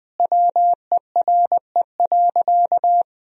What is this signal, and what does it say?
Morse code, heard as a keyed tone, about 20 wpm, 700 Hz. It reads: WERE.
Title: Sentences from Top 1000 Words - Review Part 1 - 20wpm